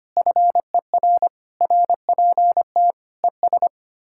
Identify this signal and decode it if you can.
Morse code, heard as a keyed tone, about 32 wpm, 700 Hz. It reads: FER RPT ES